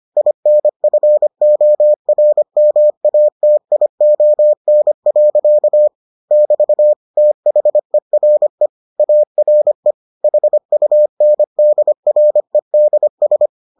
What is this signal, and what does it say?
Morse code, heard as a keyed tone, about 25 wpm, 600 Hz. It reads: INFORMATION. <BT> THERE ARE HUNDREDS